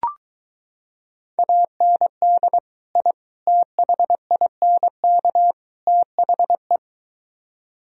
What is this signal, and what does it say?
Morse code, heard as a keyed tone, about 23 wpm, 700 Hz. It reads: AND I THINK THE